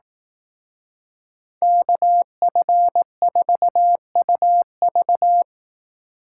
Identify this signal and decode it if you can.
Morse code, heard as a keyed tone, about 18 wpm, 700 Hz. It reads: KF4UV
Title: Speed Racing - USA 2x2 Call Signs 15wpm